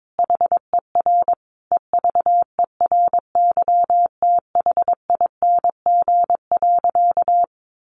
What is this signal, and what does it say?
Morse code, heard as a keyed tone, about 22 wpm, 700 Hz. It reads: HER EVERYTHING.
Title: Sentences from Top 700 Words 22wpm